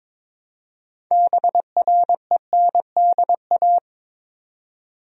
Morse code, heard as a keyed tone, about 22 wpm, 700 Hz. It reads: BRENDA
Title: USA Common Names - 22wpm